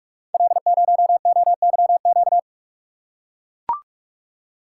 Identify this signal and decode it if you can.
Morse code, heard as a keyed tone, about 44 wpm, 700 Hz. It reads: L0OYQ
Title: Code Groups - 5 Letter:Numbers - 45wpm